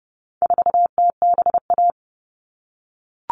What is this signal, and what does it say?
Morse code, heard as a keyed tone, about 30 wpm, 700 Hz. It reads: 4TBA
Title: All Cut Numbers - 4-Digit 30wpm